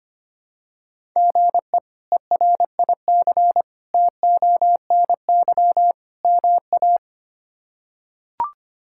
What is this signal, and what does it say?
Morse code, heard as a keyed tone, about 25 wpm, 700 Hz. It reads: GE ERIC TONY MA
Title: K1USN SST QSO Practice - 25wpm